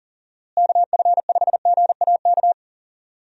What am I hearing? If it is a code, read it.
Morse code, heard as a keyed tone, about 40 wpm, 700 Hz. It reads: KF5GAK
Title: USA Call Signs - 6 Characters - Repeated 3x - 40wpm